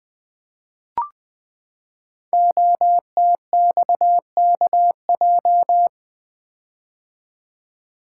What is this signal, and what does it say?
Morse code, heard as a keyed tone, about 20 wpm, 700 Hz. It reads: OTXKJ
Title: Code Groups - 5 Letter:Numbers - 20wpm